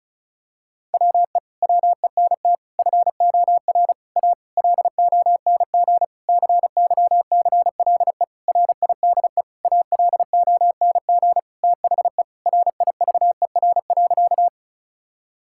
Morse code, heard as a keyed tone, about 36 wpm, 700 Hz. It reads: WE WENT FOR A LONG CYCLE RIDE ALONG THE RIVER.
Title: Sentences from Top 1000 Words - Review Part 6 - 35wpm